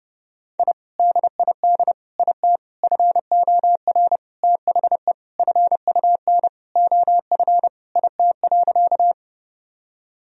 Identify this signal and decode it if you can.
Morse code, heard as a keyed tone, about 32 wpm, 700 Hz. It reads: I DID IT FOR THE FUN OF IT.
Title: Sentences from Top 1000 Words - Review Part 4 - 30wpm